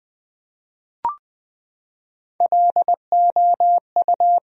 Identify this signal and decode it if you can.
Morse code, heard as a keyed tone, about 20 wpm, 700 Hz. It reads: LOU
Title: USA Common Names - Max 4 Letters - 20wpm